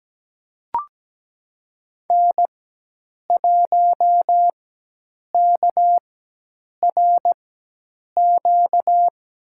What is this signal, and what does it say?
Morse code, heard as a keyed tone, about 17 wpm, spaced as for 8 wpm, 700 Hz. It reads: N1KRQ
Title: USA Call Signs - 4x Character Spacing - 17wpm